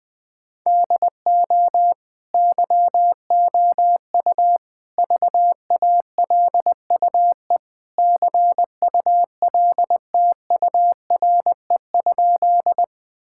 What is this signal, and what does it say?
Morse code, heard as a keyed tone, about 20 wpm, 700 Hz. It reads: DO YOU VALUE CULTURE?
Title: Sentences from Top 1000 Words - Review Part 6 - 20wpm